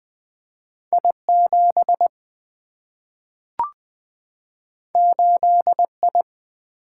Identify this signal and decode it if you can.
Morse code, heard as a keyed tone, about 20 wpm, 700 Hz. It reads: I7 8I
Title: Single-Letter Single-Number 20wpm